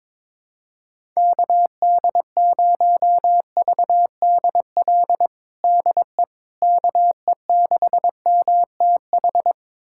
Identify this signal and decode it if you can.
Morse code, heard as a keyed tone, about 22 wpm, 700 Hz. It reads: KD0VDL DE KE6MTH
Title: USA Callsign DE Callsign - 6 Characters - 22wpm